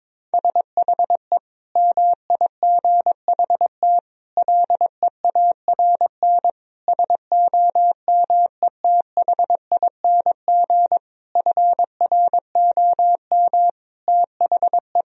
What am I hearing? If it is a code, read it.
Morse code, heard as a keyed tone, about 21 wpm, 700 Hz. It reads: SHE MIGHT LEARN SOMETHING FROM THE
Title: Sentences from Top 1000 Words - Review Part 4 - 22wpm